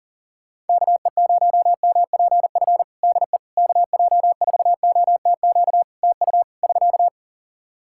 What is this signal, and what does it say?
Morse code, heard as a keyed tone, about 40 wpm, 700 Hz. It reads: KE0MPF DE KJ4OTQ TU <SK>
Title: Practice QSO - Exchange 4 - Ending - 40wpm